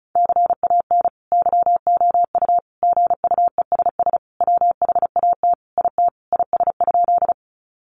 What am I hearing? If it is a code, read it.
Morse code, heard as a keyed tone, about 34 wpm, 700 Hz. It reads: CAN YOU GUESS WHAT IT IS?
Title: Sentences from Top 1500 Words 35wpm